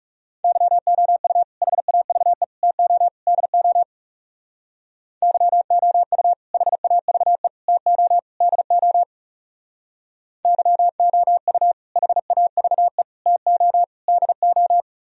Morse code, heard as a keyed tone, about 35 wpm, 700 Hz. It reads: YOU HAVE TO DO YOU HAVE TO DO YOU HAVE TO DO
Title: Speed-Racing - Sets of 4 Words 30wpm